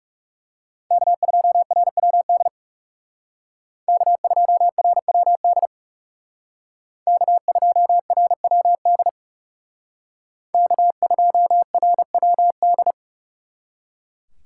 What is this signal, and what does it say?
Morse code, heard as a keyed tone, about 36 wpm, 700 Hz. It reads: K2RWD K2RWD K2RWD K2RWD
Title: Speed Racing - USA 1x3 Call Signs 30wpm